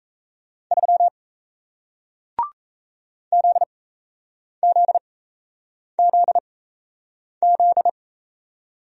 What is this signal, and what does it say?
Morse code, heard as a keyed tone, about 37 wpm, spaced as for 8 wpm, 700 Hz. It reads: 3 ZZZZ